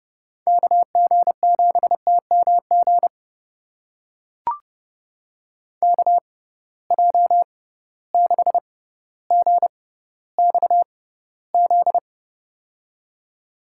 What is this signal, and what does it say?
Morse code, heard as a keyed tone, about 32 wpm, spaced as for 16 wpm, 700 Hz. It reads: KG7TMG KJ6GXZ
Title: USA Call Signs - 6 Characters - 6x Character Spacing - 30wpm